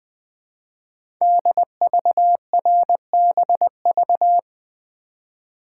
Morse code, heard as a keyed tone, about 20 wpm, 700 Hz. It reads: DVRBV
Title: Code Groups - 5 Letters - 20wpm